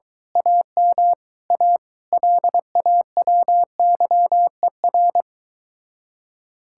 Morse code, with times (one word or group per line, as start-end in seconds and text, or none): 0.4-1.1 AM
1.5-1.8 A
2.1-5.2 LAWYER